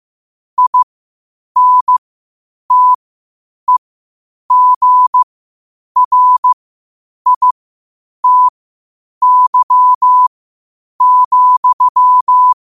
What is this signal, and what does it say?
Morse code, heard as a keyed tone, about 15 wpm, spaced as for 8 wpm, 1000 Hz. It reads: INTEGRITY,
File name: 8wpm